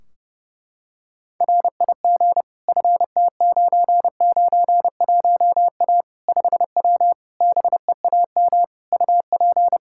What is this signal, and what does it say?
Morse code, heard as a keyed tone, about 32 wpm, 700 Hz. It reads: RIG FT991A 5W BEAM UP
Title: Practice QSO - Exchange 2 - Info - 30wpm